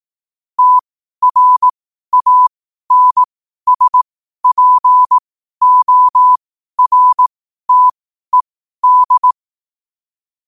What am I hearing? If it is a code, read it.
Morse code, heard as a keyed tone, about 18 wpm, spaced as for 12 wpm, 1000 Hz. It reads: TRANSPORTED
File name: qotdfiles / 12wpm